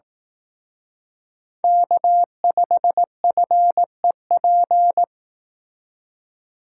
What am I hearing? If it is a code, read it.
Morse code, heard as a keyed tone, about 18 wpm, 700 Hz. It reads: K5FEP
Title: Speed Racing - USA 1x3 Call Signs 15wpm